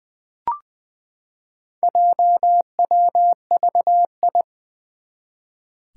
Morse code, heard as a keyed tone, about 21 wpm, 700 Hz. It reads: JWVI
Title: Four Challenging Letter-Number Combinations 20wpm